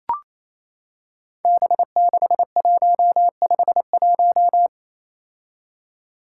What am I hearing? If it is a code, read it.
Morse code, heard as a keyed tone, about 28 wpm, 700 Hz. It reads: B6151
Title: Five Challenging Letter-Number Combinations 28wpm